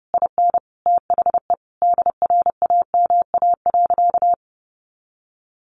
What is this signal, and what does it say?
Morse code, heard as a keyed tone, about 30 wpm, 700 Hz. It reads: IN THE DRAMA.
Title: Sentences from Top 1500 Words 30wpm